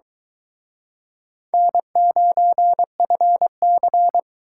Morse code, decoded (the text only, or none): N9FC